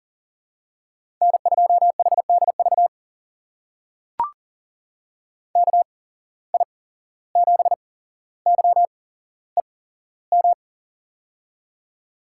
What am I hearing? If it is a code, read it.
Morse code, heard as a keyed tone, about 41 wpm, spaced as for 16 wpm, 700 Hz. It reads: N2HDV KI7YEM
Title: USA Call Signs - 8x Character Spacing - 40wpm